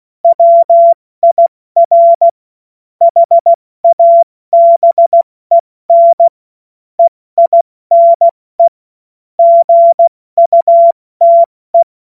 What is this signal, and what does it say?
Morse code, heard as a keyed tone, about 16 wpm, 680 Hz. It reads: WIR HABEN EINE GUTE